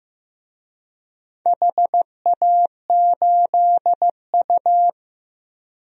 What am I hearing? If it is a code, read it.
Morse code, heard as a keyed tone, about 15 wpm, 700 Hz. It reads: HA8U
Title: DXCC-100 Call Signs - 4 Characters - Repeated 2x - 15wpm